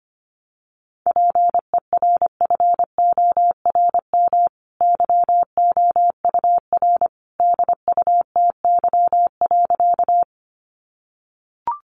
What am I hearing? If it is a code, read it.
Morse code, heard as a keyed tone, about 25 wpm, 700 Hz. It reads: PERFORM YOUR DUTY.
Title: Sentences from Top 1500 Words 25wpm